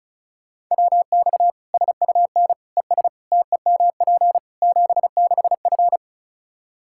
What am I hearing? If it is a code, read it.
Morse code, heard as a keyed tone, about 34 wpm, 700 Hz. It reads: WX SUN ES TEMP 76F
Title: K1USN SST QSO Practice 35wpm